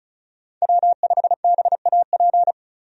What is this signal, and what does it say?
Morse code, heard as a keyed tone, about 34 wpm, 700 Hz. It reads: W5BAP
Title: USA Call Signs - 5 Characters - 35wpm